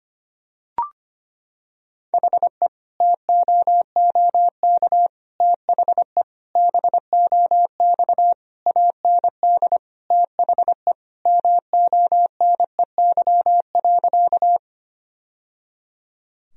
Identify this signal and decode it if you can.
Morse code, heard as a keyed tone, about 25 wpm, 700 Hz. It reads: HE TOOK THE BOX AND THE MONEY.
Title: Sentences from Top 1000 Words - Review Part 2 - 25wpm